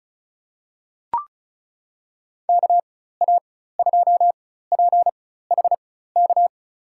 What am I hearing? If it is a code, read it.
Morse code, heard as a keyed tone, about 36 wpm, spaced as for 16 wpm, 700 Hz. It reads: KA2PHK
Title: USA Call Signs - 6 Characters - 4x Character Spacing - 35wpm